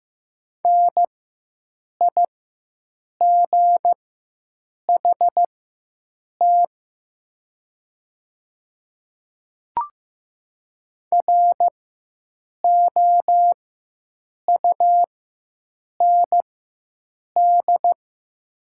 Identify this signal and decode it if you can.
Morse code, heard as a keyed tone, about 15 wpm, spaced as for 7 wpm, 700 Hz. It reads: NIGHT ROUND
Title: 4x Character Spacing - Top 300 Words - 15wpm